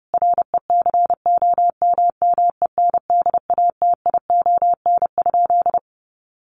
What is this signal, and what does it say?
Morse code, heard as a keyed tone, about 28 wpm, 700 Hz. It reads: RECOMMENDATION?
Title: Sentences from Top 1400 Words 30wpm